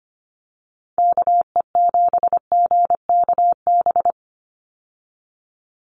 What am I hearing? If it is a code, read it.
Morse code, heard as a keyed tone, about 25 wpm, 700 Hz. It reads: KE7GKB
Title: USA Call Signs 25wpm